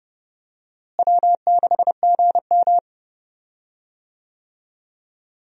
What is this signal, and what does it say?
Morse code, heard as a keyed tone, about 30 wpm, 700 Hz. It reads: W6GM
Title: USA Call Signs - 4 Characters - Repeated 2x - 30wpm